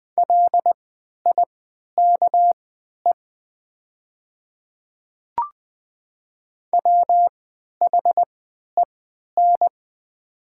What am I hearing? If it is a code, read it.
Morse code, heard as a keyed tone, about 20 wpm, spaced as for 11 wpm, 700 Hz. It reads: LIKE WHEN
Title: 3x Character Spacing - Top 100 Words - 20wpm